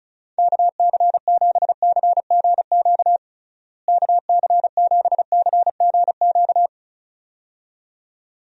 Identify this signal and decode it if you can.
Morse code, heard as a keyed tone, about 36 wpm, 700 Hz. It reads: KC7CGQ KC7CGQ
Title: USA Call Signs - 6 Characters - Repeated 2x - 35wpm